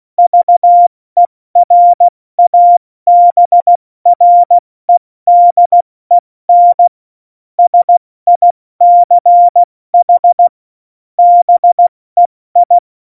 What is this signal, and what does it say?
Morse code, heard as a keyed tone, about 16 wpm, 700 Hz. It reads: VERABREDEN SICH BEI
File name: Morsekurs / Witze / Witz 27 March 2018, 70 F80